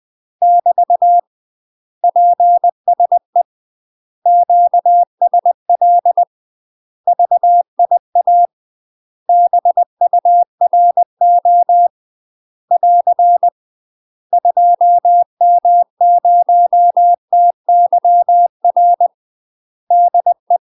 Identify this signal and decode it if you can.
Morse code, heard as a keyed tone, about 20 wpm, 700 Hz. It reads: <BT> PSE QSL VIA BURO <AR> 2M0TYR DE
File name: UK MASTER QSOs TRAINING / 20WPM / UK MASTER QSOs 0047